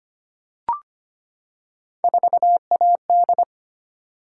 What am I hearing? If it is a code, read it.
Morse code, heard as a keyed tone, about 25 wpm, 700 Hz. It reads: 4AD
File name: Three Letter-Number Combinations - 25wpm